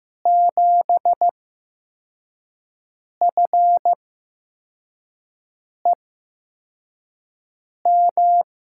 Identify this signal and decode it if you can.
Morse code, heard as a keyed tone, about 15 wpm, spaced as for 4 wpm, 700 Hz. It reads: 7FEM
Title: USA Call Signs - 5 Characters - 8x Character Spacing - 15wpm